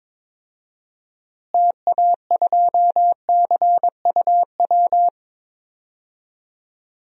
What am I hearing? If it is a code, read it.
Morse code, heard as a keyed tone, about 22 wpm, 700 Hz. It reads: TA2CUW
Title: DXCC-100 Call Signs - 6 Characters - Repeated 2x - 22wpm